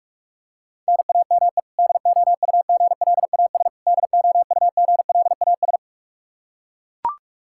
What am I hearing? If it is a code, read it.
Morse code, heard as a keyed tone, about 41 wpm, 700 Hz. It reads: NAME DOUGLAS DOUGLAS
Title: Practice QSO - Exchange 1 - Beginning - 45wpm